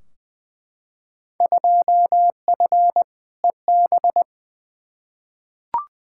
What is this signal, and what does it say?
Morse code, heard as a keyed tone, about 20 wpm, 700 Hz. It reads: 2F EB
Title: Field Day Exchanges Encore - 20wpm